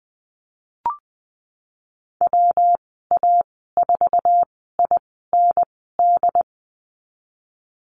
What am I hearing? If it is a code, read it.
Morse code, heard as a keyed tone, about 19 wpm, spaced as for 14 wpm, 700 Hz. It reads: WA4IND
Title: 128 Lesson 21g - Number 6 - Call Signs 20wpm